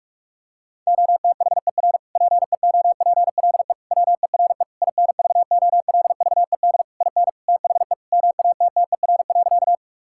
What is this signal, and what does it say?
Morse code, heard as a keyed tone, about 46 wpm, 700 Hz. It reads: OTHER PEOPLE WERE INVOLVED IN THE MATTER.
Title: Sentences from Top 1000 Words - Review Part 5 - 45wpm